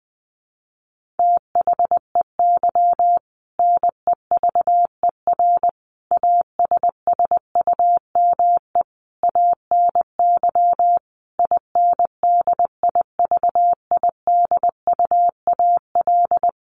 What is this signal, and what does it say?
Morse code, heard as a keyed tone, about 20 wpm, 700 Hz. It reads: THEY NEVER ASSUME ANY INDIVIDUAL
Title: Sentences from Top 1500 Words 20wpm